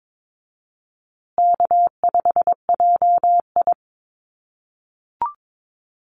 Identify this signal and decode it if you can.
Morse code, heard as a keyed tone, about 22 wpm, 700 Hz. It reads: K5JI
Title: USA Call Signs Repeated 3x 22wpm